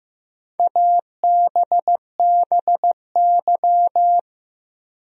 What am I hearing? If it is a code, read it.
Morse code, heard as a keyed tone, about 15 wpm, 700 Hz. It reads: ABBY
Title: USA Common Names - 15wpm